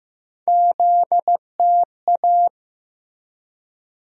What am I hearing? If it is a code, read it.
Morse code, heard as a keyed tone, about 15 wpm, 700 Hz. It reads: ZTA
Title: USA Call Sign Postfixes - 3 Letters 15wpm